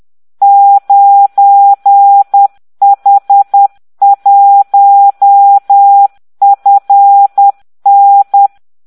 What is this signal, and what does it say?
Morse code, heard as a keyed tone, about 10 wpm, 800 Hz. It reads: 9H1FN